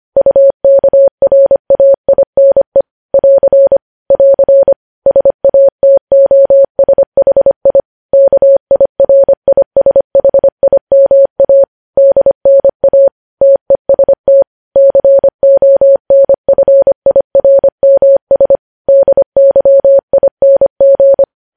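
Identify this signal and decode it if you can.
Morse code, heard as a keyed tone, about 25 wpm, 550 Hz. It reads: UKRAINE <AR> <AR> SATOSHI KIRISHIMA DNA TEST CONFIRMS DYING
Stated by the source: News headlines sent in Morse code at 25 wpm.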